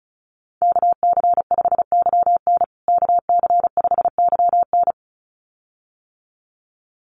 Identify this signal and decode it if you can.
Morse code, heard as a keyed tone, about 34 wpm, 700 Hz. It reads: KC5YN KC5YN
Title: USA Call Signs Repeated 2x 35wpm